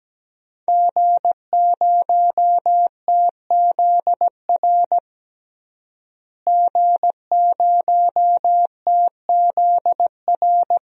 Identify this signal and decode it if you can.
Morse code, heard as a keyed tone, about 17 wpm, 700 Hz. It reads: G0TZR G0TZR
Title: DXCC-100 Call Signs - 5 Characters - Repeated 2x - 17wpm